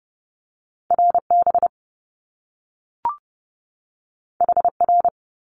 Morse code, heard as a keyed tone, about 30 wpm, 700 Hz. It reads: RB HR
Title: Top Two Letter Combinations in Words 30wpm